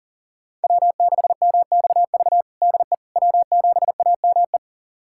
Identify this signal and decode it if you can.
Morse code, heard as a keyed tone, about 37 wpm, 700 Hz. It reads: W6MXV DE W7AME
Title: USA Callsign DE Callsign - 5 Characters - 40wpm